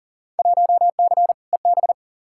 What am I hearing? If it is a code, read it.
Morse code, heard as a keyed tone, about 40 wpm, 700 Hz. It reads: 1C EB
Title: Field Day Exchanges Encore - 40wpm